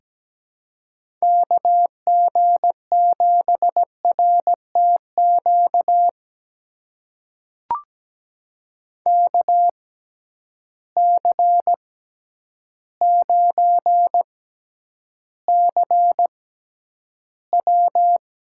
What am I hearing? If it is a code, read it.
Morse code, heard as a keyed tone, about 17 wpm, spaced as for 9 wpm, 700 Hz. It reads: KG7RTQ KC9CW
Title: USA Call Signs - 6x Character Spacing - 17wpm